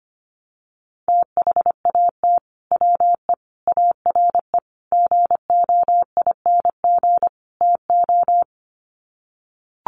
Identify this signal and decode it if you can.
Morse code, heard as a keyed tone, about 25 wpm, 700 Hz. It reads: THAT WE ARE GOING TO
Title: Sets of 5 Words Encore 25wpm